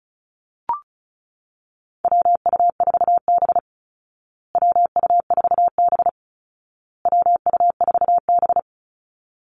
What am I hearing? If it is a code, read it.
Morse code, heard as a keyed tone, about 34 wpm, 700 Hz. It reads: WU4B WU4B WU4B
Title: Call Signs Repeated 2x - 35wpm